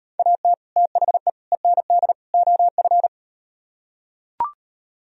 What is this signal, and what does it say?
Morse code, heard as a keyed tone, about 38 wpm, 700 Hz. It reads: AT THE END OF